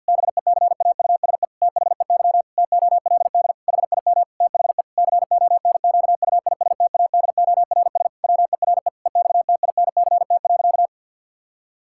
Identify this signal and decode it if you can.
Morse code, heard as a keyed tone, about 50 wpm, 700 Hz. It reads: BECAUSE THEY TOLD HIM THE CONQUISTADORS WERE EXTINCT.
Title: Pangram Challenge v2 - 50wpm